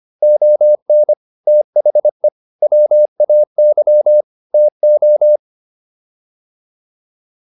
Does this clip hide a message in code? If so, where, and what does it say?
Yes: Morse code from 0.2 to 5.4 s, heard as a keyed tone, about 25 wpm, 600 Hz. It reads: ON THE WAY TO